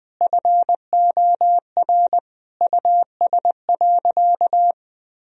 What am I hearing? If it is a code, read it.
Morse code, heard as a keyed tone, about 20 wpm, 700 Hz. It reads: FOR US.
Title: Sentences from Top 1000 Words - Review Part 2 - 20wpm